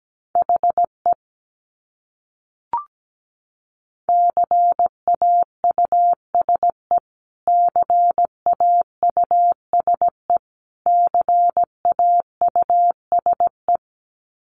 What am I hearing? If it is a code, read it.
Morse code, heard as a keyed tone, about 17 wpm, 700 Hz. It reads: HE CAUSE CAUSE CAUSE
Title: Top 200 Words Repeated 17wpm